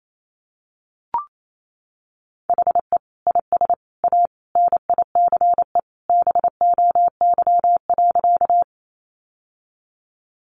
Morse code, heard as a keyed tone, about 28 wpm, 700 Hz. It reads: HE IS A NICE BOY.
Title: Sentences from Top 1100 Words 28wpm